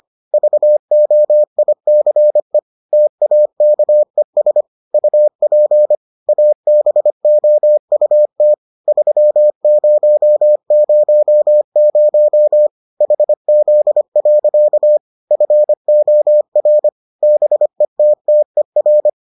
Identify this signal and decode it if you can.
Morse code, heard as a keyed tone, about 25 wpm, 600 Hz. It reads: VOICE TAKES UP ABOUT 3000 HZ. FOR BETTER